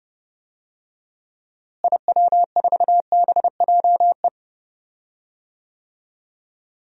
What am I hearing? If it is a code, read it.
Morse code, heard as a keyed tone, about 30 wpm, 700 Hz. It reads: IW4BJE
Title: DXCC-100 Call Signs - 6 Characters - Repeated 2x - 30wpm